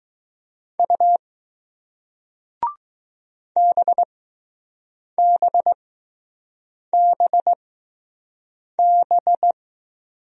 Morse code, heard as a keyed tone, about 20 wpm, spaced as for 6 wpm, 700 Hz. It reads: U BBBB